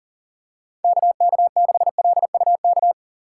Morse code, heard as a keyed tone, about 40 wpm, 700 Hz. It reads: KK6LUK
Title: USA Call Signs - 6 Characters - 4x Character Spacing - 40wpm